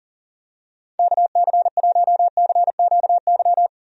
Morse code, heard as a keyed tone, about 40 wpm, 700 Hz. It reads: KC1CQY
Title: USA Call Signs - 6 Characters - 40wpm